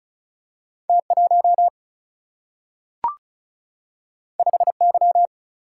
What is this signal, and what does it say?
Morse code, heard as a keyed tone, about 36 wpm, 700 Hz. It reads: T1 5Y
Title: Single-Letter Single-Number 35wpm